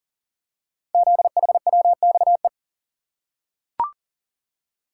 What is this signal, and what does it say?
Morse code, heard as a keyed tone, about 42 wpm, 700 Hz. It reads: ZHWXE
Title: Code Groups - 5 Letters - 40wpm